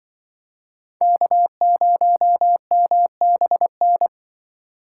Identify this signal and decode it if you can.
Morse code, heard as a keyed tone, about 23 wpm, 700 Hz. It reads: K0MBN
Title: Speed Racing - USA 1x3 Call Signs 20wpm